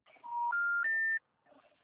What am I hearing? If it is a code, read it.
Special information tone.